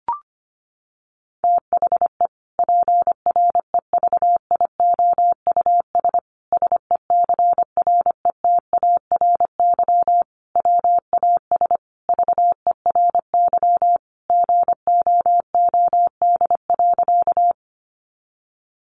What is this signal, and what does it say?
Morse code, heard as a keyed tone, about 25 wpm, 700 Hz. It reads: THE PREVIOUS SECRETARY WAS VERY GOOD.
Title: Sentences from Top 1500 Words 25wpm